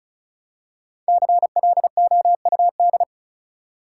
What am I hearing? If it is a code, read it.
Morse code, heard as a keyed tone, about 34 wpm, 700 Hz. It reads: CLOUD
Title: Five Letter Words - 35wpm